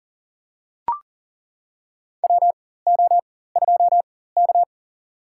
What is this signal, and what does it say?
Morse code, heard as a keyed tone, about 37 wpm, spaced as for 18 wpm, 700 Hz. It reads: WO2K
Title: USA Call Signs - 4x Character Spacing - 40wpm